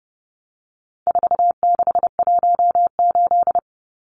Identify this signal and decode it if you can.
Morse code, heard as a keyed tone, about 28 wpm, 700 Hz. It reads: 4618
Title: Numbers - 4-Digit 30wpm